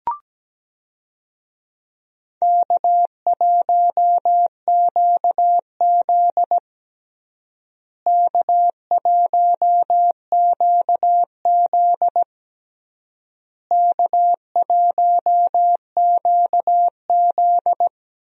Morse code, heard as a keyed tone, about 17 wpm, 700 Hz. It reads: K1QZ K1QZ K1QZ
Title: USA Call Signs - 4 Characters - Repeated 3x - 17wpm